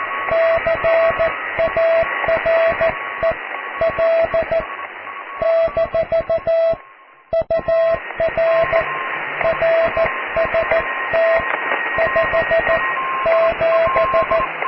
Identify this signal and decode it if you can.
Morse code, heard as a keyed tone, about 14 wpm, 660 Hz, with radio noise. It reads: CAREL - UR RST 57
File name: cw / QSO+QRM